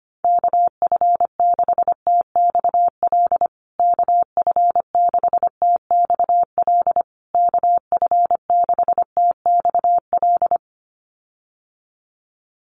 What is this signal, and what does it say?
Morse code, heard as a keyed tone, about 25 wpm, 700 Hz. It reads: KF6TXL KF6TXL KF6TXL
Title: USA Call Signs Repeated 3x 25wpm